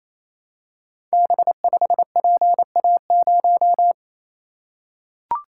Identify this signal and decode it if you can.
Morse code, heard as a keyed tone, about 28 wpm, 700 Hz. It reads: B5PA0
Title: Code Groups - 5 Letter:Numbers - 28wpm